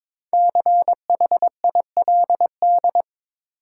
Morse code, heard as a keyed tone, about 23 wpm, 700 Hz. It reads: CHILD
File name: 5 Letter Words - First 3 Letters Given - 22wpm